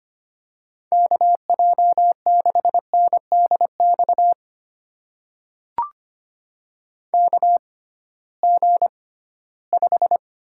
Morse code, heard as a keyed tone, about 25 wpm, spaced as for 16 wpm, 700 Hz. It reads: KJ6NDX KG5
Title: USA Call Signs - 6 Characters - 6x Character Spacing - 25wpm